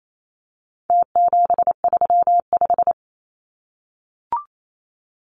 Morse code, heard as a keyed tone, about 28 wpm, 700 Hz. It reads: T735
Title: Numbers - Contest Style - 4-Digit 28wpm